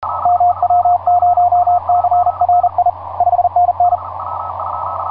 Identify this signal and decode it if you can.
Morse code, heard as a keyed tone, about 33 wpm, 710 Hz, with radio noise. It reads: MW0CRI 5NN
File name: j38xx mw0cri etc 15m cw 18feb2010 1532